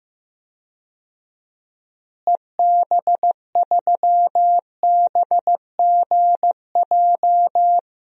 Morse code, heard as a keyed tone, about 15 wpm, 700 Hz. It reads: EB3BGJ
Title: DXCC-100 Call Signs - 6 Characters - Repeated 3x - 15wpm